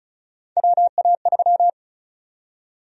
Morse code, heard as a keyed tone, about 36 wpm, 700 Hz. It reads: WA3
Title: USA Call Sign Prefixes - 35wpm